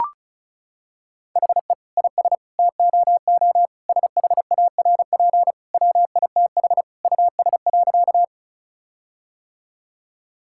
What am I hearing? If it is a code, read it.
Morse code, heard as a keyed tone, about 36 wpm, 700 Hz. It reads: HE IS TOO SHARP WITH US.
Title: Sentences from Top 1600 Words 35wpm